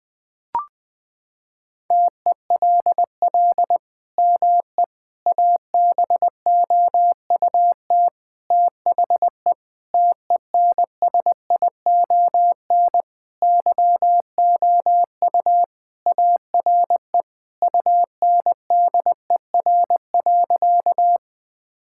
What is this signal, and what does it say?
Morse code, heard as a keyed tone, about 20 wpm, 700 Hz. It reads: TELL ME ABOUT THE TENSION YOU ARE UNDER.
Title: Sentences from Top 1600 Words 20wpm